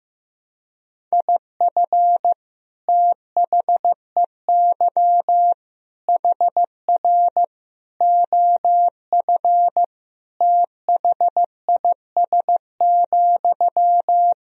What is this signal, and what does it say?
Morse code, heard as a keyed tone, about 14 wpm, 700 Hz. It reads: IF THEY HR OF THIS,
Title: Sentences from Top 400 Words and Abbreviations 15wpm